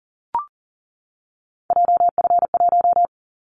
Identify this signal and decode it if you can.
Morse code, heard as a keyed tone, about 40 wpm, 700 Hz. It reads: JF1
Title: Callsign Prefixes for Top 25 DXCC - 40wpm